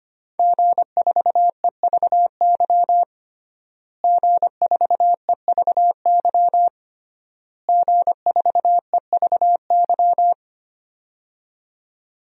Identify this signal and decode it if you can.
Morse code, heard as a keyed tone, about 25 wpm, 700 Hz. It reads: G4EVY G4EVY G4EVY
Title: DXCC-100 Call Signs - 5 Characters - Repeated 3x - 25wpm